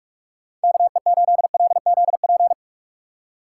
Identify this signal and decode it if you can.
Morse code, heard as a keyed tone, about 46 wpm, 700 Hz. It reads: KE8LZP
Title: USA Call Signs - 6 Characters - 4x Character Spacing - 45wpm